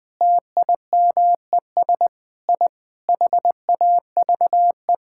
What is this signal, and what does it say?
Morse code, heard as a keyed tone, about 21 wpm, 700 Hz. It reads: TIMES I HAVE
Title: Sentences with 6-Word Phrases - Spoken Beginning - 20wpm